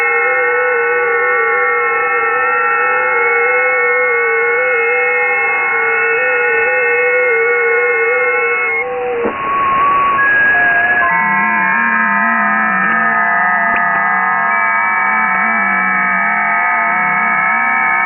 FT8.
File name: Radio / SDR Audio / gqrx